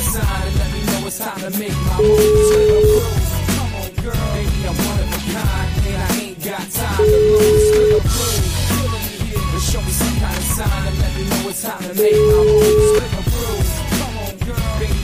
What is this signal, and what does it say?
Ringback tone (Europe, 425 Hz), with noise.